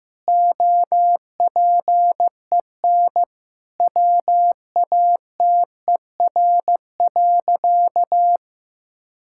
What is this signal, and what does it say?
Morse code, heard as a keyed tone, about 16 wpm, 700 Hz. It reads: OPEN WATER.
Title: Sentences from Top 1000 Words - Review Part 6 - 15wpm